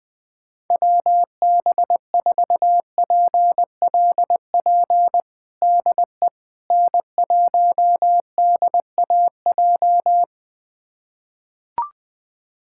Morse code, heard as a keyed tone, about 20 wpm, 690 Hz. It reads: WB4PLP DE N1DAJ
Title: USA Callsign DE Callsign 20wpm